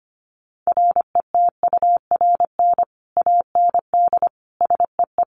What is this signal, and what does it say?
Morse code, heard as a keyed tone, about 25 wpm, 700 Hz. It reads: RETURN AND SEE
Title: Sentences from Top 800 Words 25wpm